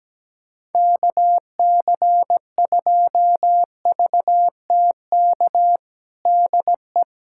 Morse code, heard as a keyed tone, about 17 wpm, 700 Hz. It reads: KC2VTK DE
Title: USA Callsign DE Callsign 17wpm